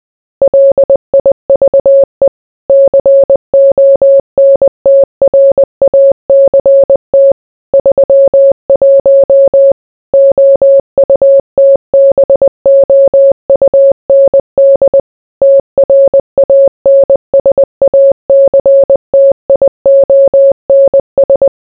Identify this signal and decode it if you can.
Morse code, heard as a keyed tone, about 20 wpm, 550 Hz. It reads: LIVE CONTRACT 31 OUTBOUND TRANSACTIONS